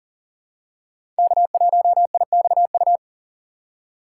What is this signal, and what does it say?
Morse code, heard as a keyed tone, about 39 wpm, 700 Hz. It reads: K1IXU
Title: USA Call Signs - 5 Characters - 2x Character Spacing - 40wpm